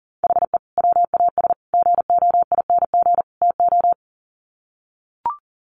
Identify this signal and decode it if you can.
Morse code, heard as a keyed tone, about 40 wpm, 700 Hz. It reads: HE WAS GOING TO
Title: Sets of 4 Words 40wpm